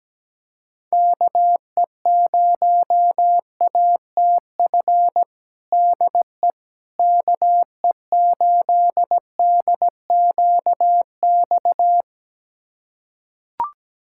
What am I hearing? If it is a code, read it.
Morse code, heard as a keyed tone, about 17 wpm, 700 Hz. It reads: KE0ATF DE KE8DQX
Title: USA Callsign DE Callsign 17wpm